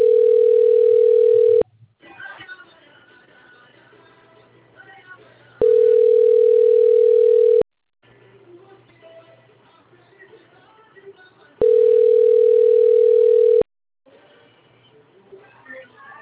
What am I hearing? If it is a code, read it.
Ringback tone (North America), 440 + 480 Hz, with speech.
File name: out- - - - - .77275